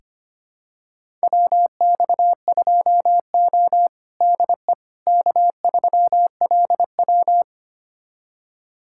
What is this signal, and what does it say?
Morse code, heard as a keyed tone, about 25 wpm, 700 Hz. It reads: WX2O DE K3LW
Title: USA Callsign DE Callsign - 4 Characters - 25wpm